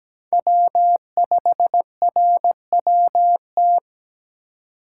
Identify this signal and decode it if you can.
Morse code, heard as a keyed tone, about 17 wpm, 700 Hz. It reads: W5RWT
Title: USA Call Signs - 4x Character Spacing - 17wpm